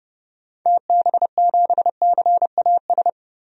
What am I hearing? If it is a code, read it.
Morse code, heard as a keyed tone, about 32 wpm, 700 Hz. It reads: TB7CAS
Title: DXCC-100 Call Signs - 6 Characters - 30wpm